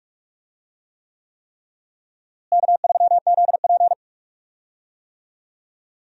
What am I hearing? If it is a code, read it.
Morse code, heard as a keyed tone, about 46 wpm, 700 Hz. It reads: K3ZP
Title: USA Call Signs - 4 Characters - Repeated 3x - 45wpm